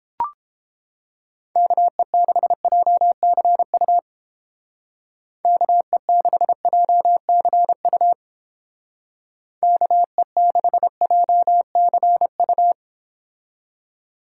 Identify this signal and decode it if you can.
Morse code, heard as a keyed tone, about 30 wpm, 700 Hz. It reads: KE6JCU KE6JCU KE6JCU